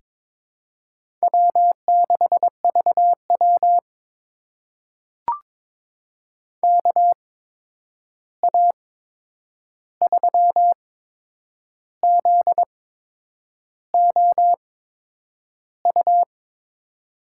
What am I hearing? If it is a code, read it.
Morse code, heard as a keyed tone, about 22 wpm, spaced as for 8 wpm, 700 Hz. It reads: W6VW KA3ZOU
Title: USA Call Signs - 8x Character Spacing - 22wpm